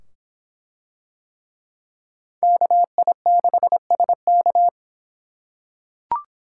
Morse code, heard as a keyed tone, about 24 wpm, 700 Hz. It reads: KI6SK